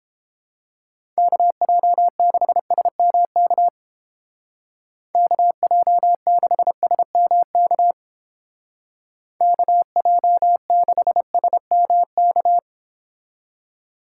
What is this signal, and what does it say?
Morse code, heard as a keyed tone, about 31 wpm, 700 Hz. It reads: KJ6SMK KJ6SMK KJ6SMK